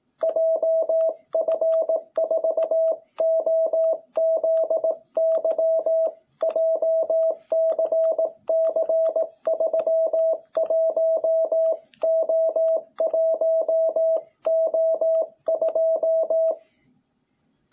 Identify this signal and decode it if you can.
Morse code, heard as a keyed tone, about 18 wpm, 660 Hz. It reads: JF4OZY JCC31O1O2
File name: jh8cbh / 2021cw